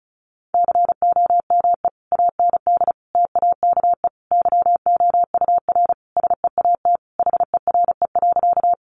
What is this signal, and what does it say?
Morse code, heard as a keyed tone, about 34 wpm, 700 Hz. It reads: COME AND TAKE YOUR SEAT HERE.
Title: Sentences from Top 1500 Words 35wpm